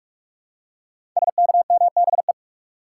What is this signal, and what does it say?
Morse code, heard as a keyed tone, about 44 wpm, 700 Hz. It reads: SKMBE
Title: Code Groups - 5 Letters - 45wpm